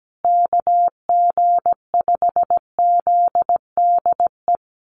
Morse code, heard as a keyed tone, about 17 wpm, 700 Hz. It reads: KG5ZDE
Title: USA Call Signs Repeated 3x 17wpm